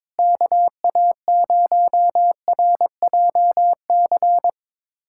Morse code, heard as a keyed tone, about 22 wpm, 700 Hz. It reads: KA0RJC
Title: USA Callsign DE Callsign - 6 Characters - 22wpm